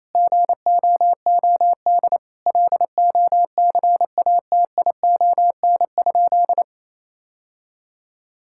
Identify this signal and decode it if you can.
Morse code, heard as a keyed tone, about 28 wpm, 700 Hz. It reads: GOOD LOCATION?
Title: Sentences from Top 1000 Words - Review Part 6 - 28wpm